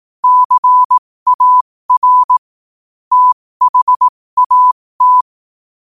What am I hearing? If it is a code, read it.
Morse code, heard as a keyed tone, about 18 wpm, spaced as for 15 wpm, 1000 Hz. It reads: CAR THAT
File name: qotdfiles / 15wpm